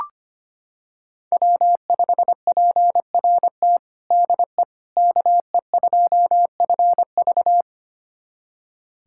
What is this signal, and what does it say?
Morse code, heard as a keyed tone, about 25 wpm, 700 Hz. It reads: W5PRT DE KE2FV
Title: USA Callsign DE Callsign - 5 Characters - 25wpm